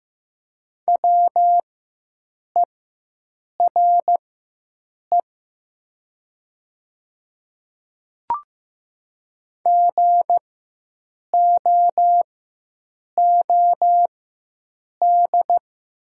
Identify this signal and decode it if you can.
Morse code, heard as a keyed tone, about 15 wpm, spaced as for 7 wpm, 700 Hz. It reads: WERE GOOD
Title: 4x Character Spacing - Top 200 Words - 15wpm